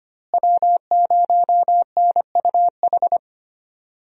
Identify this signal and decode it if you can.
Morse code, heard as a keyed tone, about 25 wpm, 700 Hz. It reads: W0NUH
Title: USA Call Signs - 5 Characters - Repeated 2x - 25wpm